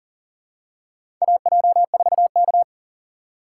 Morse code, heard as a keyed tone, about 40 wpm, 700 Hz. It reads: AJ4K
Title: USA Call Signs - 4 Characters - 6x Character Spacing - 40wpm